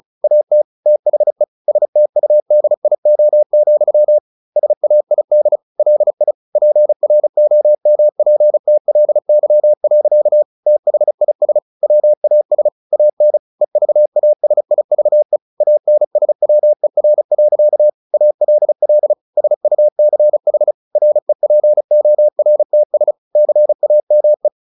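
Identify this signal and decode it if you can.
Morse code, heard as a keyed tone, about 34 wpm, 600 Hz. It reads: AT THE STUDIO, SAID LI PROMPTLY. THIS WAS AN EVASIVE ANSWER. ALL SUCH REPORTS CAME
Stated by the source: Morse practice at 35 wpm from CW med Gnister, sending Ken Follett's novel Never in English.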